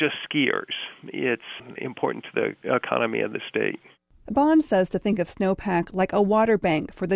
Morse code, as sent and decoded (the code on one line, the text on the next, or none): none